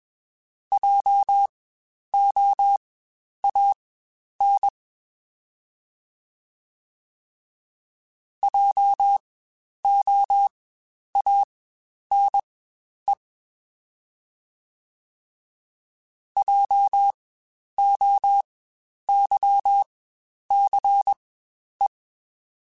Morse code, heard as a keyed tone, about 20 wpm, spaced as for 10 wpm, 800 Hz. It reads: JOAN JOANE JOYCE
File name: Spaced Alphas / Spaced Alphas mp3s / Js sp @20